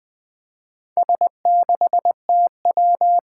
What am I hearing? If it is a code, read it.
Morse code, heard as a keyed tone, about 20 wpm, 700 Hz. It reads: S6TW
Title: Four Challenging Letter-Number Combinations 20wpm